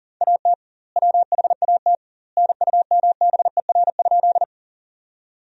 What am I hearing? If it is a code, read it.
Morse code, heard as a keyed tone, about 40 wpm, 700 Hz. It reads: AT WHAT NUMBER?
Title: 2x Word-Spacing - Sentences from Top 100 Words - 40wpm